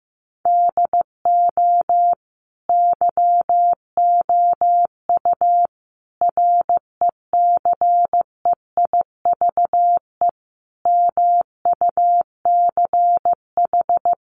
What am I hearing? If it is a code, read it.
Morse code, heard as a keyed tone, about 15 wpm, 700 Hz. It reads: DO YOU RECEIVE MUCH
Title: Sentences from Top 1000 Words 15wpm